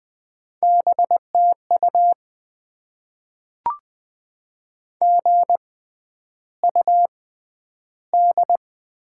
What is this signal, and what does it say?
Morse code, heard as a keyed tone, about 20 wpm, spaced as for 10 wpm, 700 Hz. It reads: BTU GUD
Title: Top 20 QSO Elements - 6x Character Spacing - 20wpm